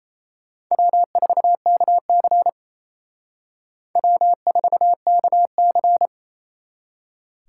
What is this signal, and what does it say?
Morse code, heard as a keyed tone, about 34 wpm, 700 Hz. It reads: W4KC W4KC
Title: Speed Racing - USA 1x2 Call Signs 28wpm